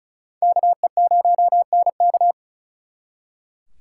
Morse code, heard as a keyed tone, about 47 wpm, spaced as for 42 wpm, 700 Hz. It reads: KE0NK
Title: Speed Racing - USA 2x2 Call Signs 35wpm